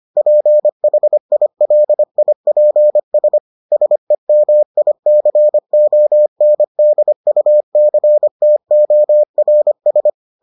Morse code, heard as a keyed tone, about 25 wpm, 600 Hz. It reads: PHILIPS SEMICONDUCTORS